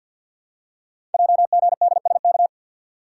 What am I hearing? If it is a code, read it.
Morse code, heard as a keyed tone, about 52 wpm, 700 Hz. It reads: JGDSK
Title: Code Groups - 5 Letter:Numbers - 50wpm